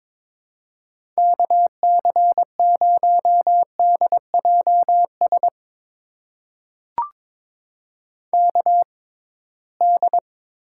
Morse code, heard as a keyed tone, about 22 wpm, spaced as for 17 wpm, 700 Hz. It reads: KC0DJS KD
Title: USA Call Signs - 6 Characters - 6x Character Spacing - 22wpm